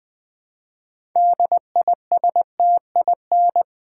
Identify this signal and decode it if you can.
Morse code, heard as a keyed tone, about 20 wpm, 700 Hz. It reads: DISTIN
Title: Build A Word - 2 Letters At A Time - 8-Letter Words - 20wpm